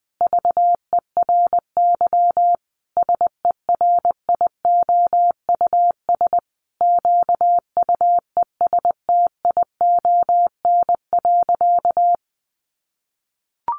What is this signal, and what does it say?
Morse code, heard as a keyed tone, about 20 wpm, 700 Hz. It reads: VERY SERIOUS QUESTION.
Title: Sentences from Top 1400 Words 20wpm